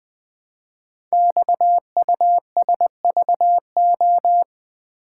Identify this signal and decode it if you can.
Morse code, heard as a keyed tone, about 20 wpm, 700 Hz. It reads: XUSVO
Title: Code Groups - 5 Letter:Numbers - 20wpm